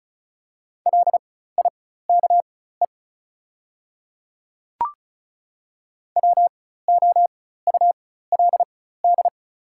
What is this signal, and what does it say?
Morse code, heard as a keyed tone, about 36 wpm, spaced as for 16 wpm, 690 Hz. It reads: LIKE WOULD
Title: 4x Character Spacing - Top 100 Words - 35wpm